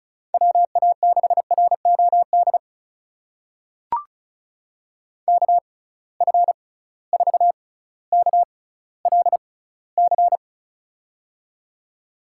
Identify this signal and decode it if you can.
Morse code, heard as a keyed tone, about 36 wpm, spaced as for 18 wpm, 700 Hz. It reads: WA6ROD KF4KLC
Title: USA Call Signs - 6 Characters - 6x Character Spacing - 35wpm